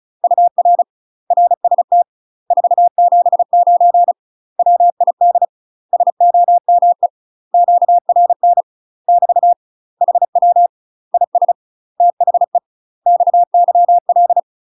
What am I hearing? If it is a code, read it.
Morse code, heard as a keyed tone, about 34 wpm, 700 Hz. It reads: UR RST 479 WID SOME QRN <BT> HW IS THE XYL
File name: UK MASTER QSOs TRAINING / 35WPM / UK MASTER QSOs 0005